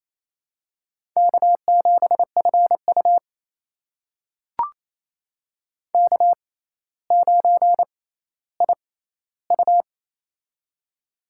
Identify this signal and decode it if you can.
Morse code, heard as a keyed tone, about 28 wpm, spaced as for 14 wpm, 700 Hz. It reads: K7FU K9IU
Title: USA Call Signs - 4 Characters - 6x Character Spacing - 28wpm